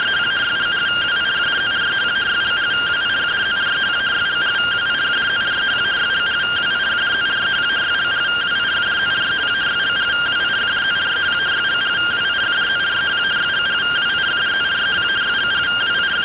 Two-tone teleprinter signal, about 170 Hz shift.